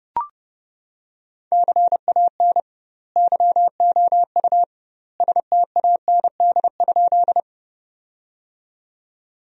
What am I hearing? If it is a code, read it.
Morse code, heard as a keyed tone, about 30 wpm, 700 Hz. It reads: CAN YOU STAND?
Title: 2x Word-Spacing - Sentences from Top 300 Words - 30wpm